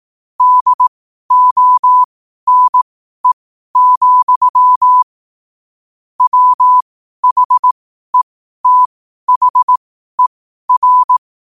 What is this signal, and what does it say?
Morse code, heard as a keyed tone, about 18 wpm, spaced as for 12 wpm, 1000 Hz. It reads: DONE, WHETHER
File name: qotdfiles / 12wpm